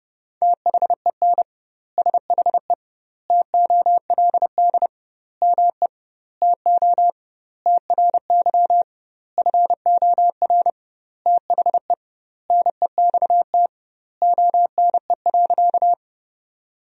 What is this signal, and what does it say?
Morse code, heard as a keyed tone, about 32 wpm, 700 Hz. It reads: THEN SHE TOLD ME TO TRY FOR THE NEXT ONE.
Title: 2x Word-Spacing - Sentences from Top 400 Words - 30wpm